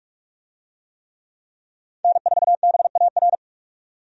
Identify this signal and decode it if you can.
Morse code, heard as a keyed tone, about 46 wpm, 700 Hz. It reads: N4BAR
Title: USA Call Signs - 5 Characters - 45wpm